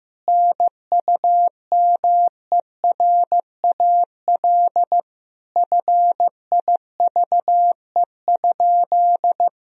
Morse code, heard as a keyed tone, about 15 wpm, 700 Hz. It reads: NUMERAL FIVE?
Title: Sentences from Top 1000 Words - Review Part 2 - 15wpm